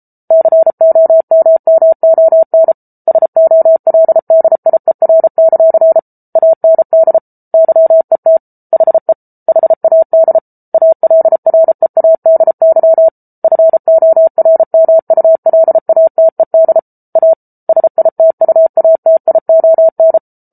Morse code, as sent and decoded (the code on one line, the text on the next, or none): -.-. --- -- -- --- -. / ... --- .-.. -.. .. .-. -.-.-. / .- -. -.. / -.-- . - / .... / .... .- -.. / .- .-.. .-. .- -.. -.-- / ..-. --- .-. -- ..- .-.. .- - . -.. / .- / ... .. - ..- .- - .. --- -.
COMMON SOLDIER; AND YET HE HAD ALREADY FORMULATED A SITUATION